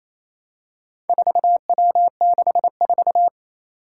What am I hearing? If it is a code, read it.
Morse code, heard as a keyed tone, about 28 wpm, 700 Hz. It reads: 4W64
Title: Four Challenging Letter-Number Combinations 28wpm